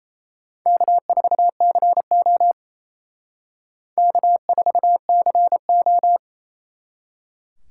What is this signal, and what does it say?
Morse code, heard as a keyed tone, about 35 wpm, 700 Hz. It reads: K4CO K4CO
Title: Speed Racing - USA Call Signs 28wpm